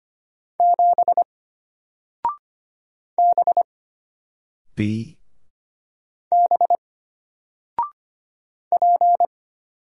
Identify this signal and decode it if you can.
Morse code, heard as a keyed tone, about 25 wpm, 700 Hz. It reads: 7 B B P
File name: Single-Letter-Number-25wpm